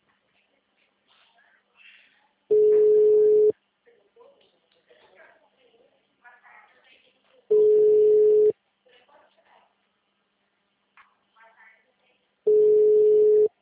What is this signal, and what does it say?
Ringback tone (Europe, 425 Hz), with speech.
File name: out- -2006- - - .350